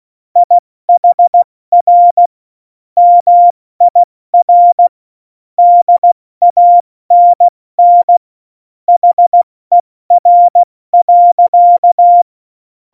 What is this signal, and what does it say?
Morse code, heard as a keyed tone, about 16 wpm, 700 Hz. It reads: IHR MIR DANN HER.